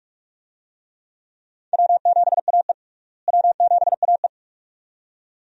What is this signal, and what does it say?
Morse code, heard as a keyed tone, about 46 wpm, 700 Hz. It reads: W7AE W7AE
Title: USA Call Signs - 4 Characters - Repeated 2x - 45wpm